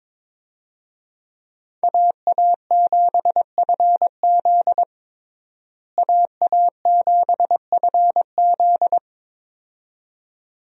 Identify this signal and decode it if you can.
Morse code, heard as a keyed tone, about 22 wpm, 700 Hz. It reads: AA7FZ AA7FZ
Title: USA Call Signs - 5 Characters - Repeated 2x - 22wpm